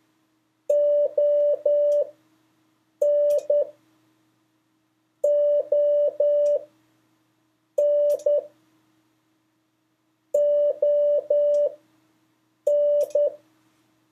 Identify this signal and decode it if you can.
Morse code, heard as a keyed tone, about 10 wpm, spaced as for 6 wpm, 580 Hz. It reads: ON ON ON
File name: 4of5-words-3